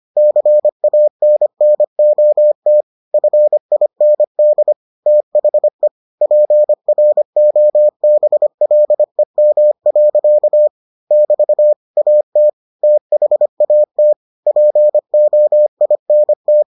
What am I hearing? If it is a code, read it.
Morse code, heard as a keyed tone, about 25 wpm, 600 Hz. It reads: CANNOT FIND THE PROBLEM. <BT> AT THAT POINT